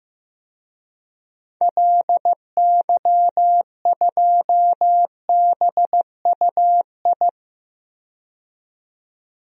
Morse code, heard as a keyed tone, about 15 wpm, 700 Hz. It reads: LY2BUI
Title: DXCC-100 Call Signs - 6 Characters - 15wpm